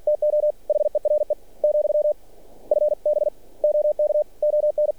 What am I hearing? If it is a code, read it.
Morse code, heard as a keyed tone, about 46 wpm, 600 Hz. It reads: TO HERE , FB OK ON
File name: archived / CRSnet